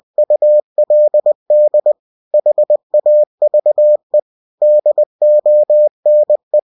Morse code, heard as a keyed tone, about 20 wpm, 600 Hz. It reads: ULD HAVE DONE
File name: cw-books / winnie-the-pooh / winnie-20wpm-page-0107